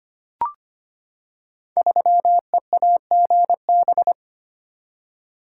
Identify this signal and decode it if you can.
Morse code, heard as a keyed tone, about 25 wpm, 700 Hz. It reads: 3EAGB
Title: Code Groups - 5 Letter:Numbers - 25wpm